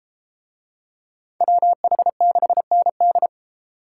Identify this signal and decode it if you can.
Morse code, heard as a keyed tone, about 30 wpm, 700 Hz. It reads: WH6ND